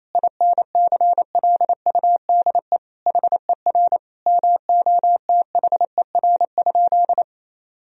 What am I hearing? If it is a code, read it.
Morse code, heard as a keyed tone, about 28 wpm, 700 Hz. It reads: INCLUDE HER MOTHER?
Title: Sentences from Top 1000 Words - Review Part 3 - 28wpm